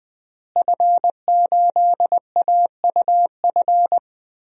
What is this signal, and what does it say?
Morse code, heard as a keyed tone, about 20 wpm, 700 Hz. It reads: F8AUF
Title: DXCC-100 Call Signs - 5 Characters - Repeated 2x - 20wpm